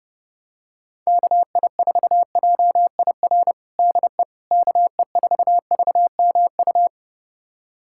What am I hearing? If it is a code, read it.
Morse code, heard as a keyed tone, about 31 wpm, 700 Hz. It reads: KI4JIR DE KE4VMU
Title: USA Callsign DE Callsign - 6 Characters - 30wpm